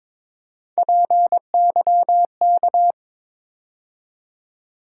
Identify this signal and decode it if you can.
Morse code, heard as a keyed tone, about 22 wpm, 700 Hz. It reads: PYK
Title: USA Call Sign Postfixes - 3 Letters 22wpm